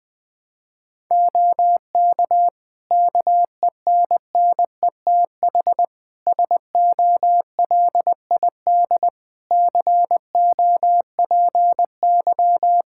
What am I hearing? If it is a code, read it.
Morse code, heard as a keyed tone, about 21 wpm, 700 Hz. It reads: OK KENNETH SOLID COPY
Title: Practice QSO - Exchange 3 - More Info - 20wpm